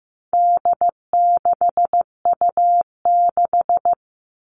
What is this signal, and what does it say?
Morse code, heard as a keyed tone, about 15 wpm, 700 Hz. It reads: D6U6
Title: All Cut Numbers - 4-Digit 15wpm